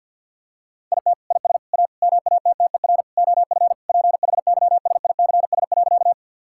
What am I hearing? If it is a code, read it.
Morse code, heard as a keyed tone, about 50 wpm, 700 Hz. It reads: IT IS A MATTER OF PHYSICS.
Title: Sentences from Top 1000 Words - Review Part 5 - 50wpm